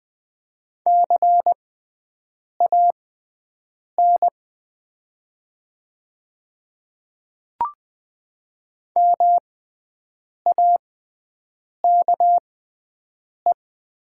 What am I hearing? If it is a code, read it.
Morse code, heard as a keyed tone, about 20 wpm, spaced as for 7 wpm, 700 Hz. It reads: CAN MAKE
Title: 6x Character Spacing - Top 100 Words - 20wpm